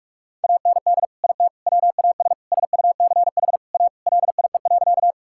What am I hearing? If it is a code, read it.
Morse code, heard as a keyed tone, about 42 wpm, 700 Hz. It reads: AND IT WAS SUCH A LIE.
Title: Sentences from Top 1600 Words 45wpm